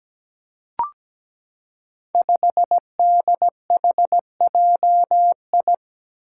Morse code, heard as a keyed tone, about 17 wpm, 700 Hz. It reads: 5DHJI
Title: Five Challenging Letter-Number Combinations 17wpm